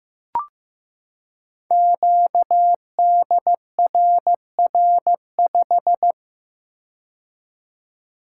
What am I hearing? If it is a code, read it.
Morse code, heard as a keyed tone, about 15 wpm, 700 Hz. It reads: QDRR5
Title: Code Groups - 5 Letter:Numbers - 15wpm